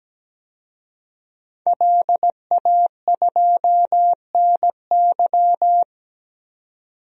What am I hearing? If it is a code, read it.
Morse code, heard as a keyed tone, about 17 wpm, 700 Hz. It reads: LA2NY
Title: DXCC-100 Call Signs - 5 Characters - 17wpm